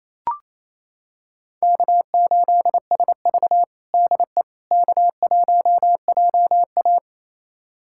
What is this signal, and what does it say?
Morse code, heard as a keyed tone, about 28 wpm, 700 Hz. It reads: K8SV DE K1JA
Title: USA Callsign DE Callsign - 4 Characters - 28wpm